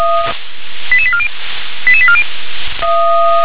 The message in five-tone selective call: 88840 88840

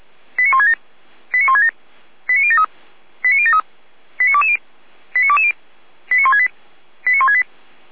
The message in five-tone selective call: 89167 89167 89073 89073 89220 89220 89168 89168

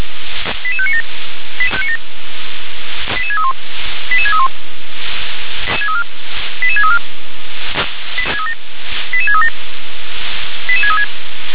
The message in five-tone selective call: 88597 88631 88631 88634 88637 88636

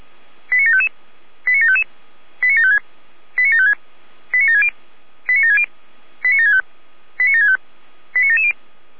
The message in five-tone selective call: 89755 89755 89756 89756 89760 89760 89765 89765 89770

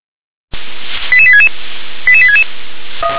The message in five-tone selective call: 88866 88866

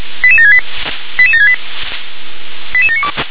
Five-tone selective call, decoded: 88768 88768 88792

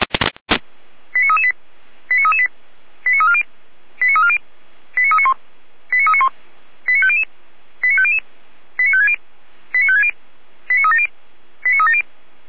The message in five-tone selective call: 89338 89338 89340 89340 89391 89391 89500 89500 89560 89560 89370 89370

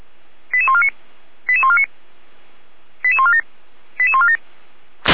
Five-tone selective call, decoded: 88149 88149 88147 88147

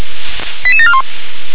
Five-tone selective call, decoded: 88741

